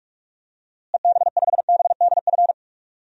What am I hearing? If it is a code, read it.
Morse code, heard as a keyed tone, about 46 wpm, 700 Hz. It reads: EB5BDF
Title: DXCC-100 Call Signs - 6 Characters - Repeated 3x - 45wpm